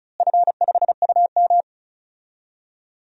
Morse code, heard as a keyed tone, about 36 wpm, 700 Hz. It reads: F5UM
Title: DXCC-100 Call Signs - 4 Characters - Repeated 3x - 35wpm